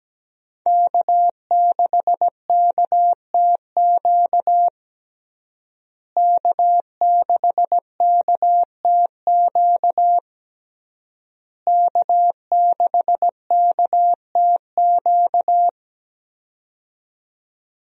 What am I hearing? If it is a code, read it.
Morse code, heard as a keyed tone, about 17 wpm, 700 Hz. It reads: K6KTQ K6KTQ K6KTQ
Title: USA Call Signs - 5 Characters - Repeated 3x - 17wpm